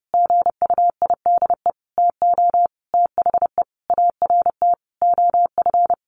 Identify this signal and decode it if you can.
Morse code, heard as a keyed tone, about 30 wpm, 700 Hz. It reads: GUIDE TO THE ART OF
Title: Sentences from Top 1200 Words 30wpm